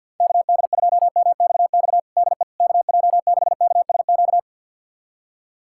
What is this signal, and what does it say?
Morse code, heard as a keyed tone, about 51 wpm, 700 Hz. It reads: KD2MXX DE KJ6KSQ
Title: USA Callsign DE Callsign 50wpm